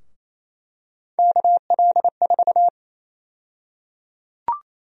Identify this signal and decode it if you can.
Morse code, heard as a keyed tone, about 28 wpm, 700 Hz. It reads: KL4
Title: USA Call Sign Prefixes - Repeated 2x - 28wpm